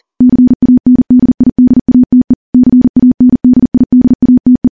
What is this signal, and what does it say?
Morse code, heard as a keyed tone, about 40 wpm, 250 Hz. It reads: CANDIDATE CANDIDATE
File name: morse / morse-training-40WPM